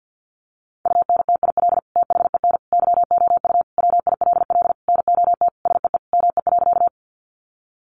Morse code, heard as a keyed tone, about 48 wpm, 700 Hz. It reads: UNTIL THEN YOU WILL NOT SEE ME.
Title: Sentences from Top 600 Words 50wpm